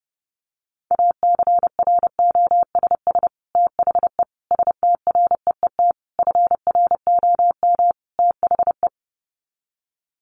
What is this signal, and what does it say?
Morse code, heard as a keyed tone, about 29 wpm, 700 Hz. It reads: ACROSS THE STREET FROM THE
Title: Sets of 5 Words Encore 30wpm